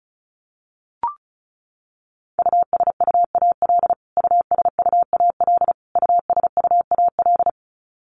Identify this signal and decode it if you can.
Morse code, heard as a keyed tone, about 34 wpm, 700 Hz. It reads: USUAL USUAL USUAL
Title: Top 500 Words Repeated 35wpm